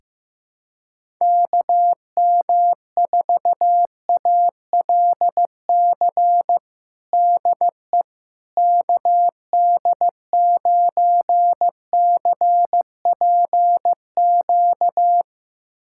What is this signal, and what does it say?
Morse code, heard as a keyed tone, about 15 wpm, 700 Hz. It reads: KM4ALC DE KD9CPQ
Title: USA Callsign DE Callsign 15wpm